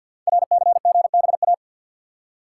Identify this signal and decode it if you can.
Morse code, heard as a keyed tone, about 48 wpm, 700 Hz. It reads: RCGBA
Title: Code Groups - 5 Letters - 50wpm